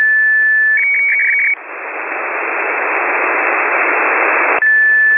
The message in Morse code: none